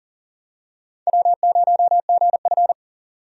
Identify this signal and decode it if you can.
Morse code, heard as a keyed tone, about 37 wpm, 700 Hz. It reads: W0GF
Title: USA Call Signs - 4 Characters - 40wpm